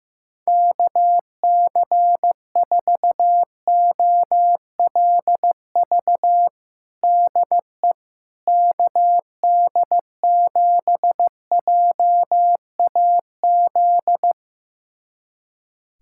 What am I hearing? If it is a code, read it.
Morse code, heard as a keyed tone, about 15 wpm, 700 Hz. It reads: KC4OLV DE KD7JAZ
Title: USA Callsign DE Callsign - 6 Characters - 15wpm